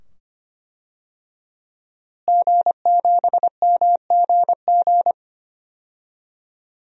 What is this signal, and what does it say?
Morse code, heard as a keyed tone, about 25 wpm, 700 Hz. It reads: G7MGG
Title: DXCC-100 Call Signs - 5 Characters - Repeated 2x - 25wpm